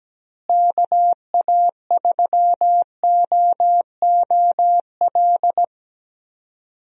Morse code, heard as a keyed tone, about 17 wpm, 700 Hz. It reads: KA3OOL
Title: USA Call Signs - 6 Characters - Repeated 3x - 17wpm